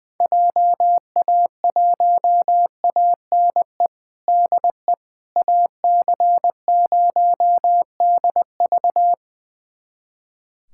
Morse code, heard as a keyed tone, about 20 wpm, 700 Hz. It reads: JA1ANE DE AC0DV
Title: USA Callsign DE Callsign 20wpm